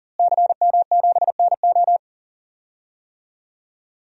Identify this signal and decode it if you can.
Morse code, heard as a keyed tone, about 40 wpm, 700 Hz. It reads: CM7NO
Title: DXCC-100 Call Signs - 5 Characters - Repeated 3x - 40wpm